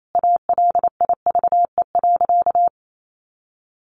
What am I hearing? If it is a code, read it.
Morse code, heard as a keyed tone, about 28 wpm, 700 Hz. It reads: ALIVE.
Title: A Sound Of Thunder 28wpm